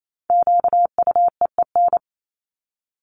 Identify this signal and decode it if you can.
Morse code, heard as a keyed tone, about 28 wpm, 700 Hz. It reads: QUEEN
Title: Top 1751-2000 Words 28wpm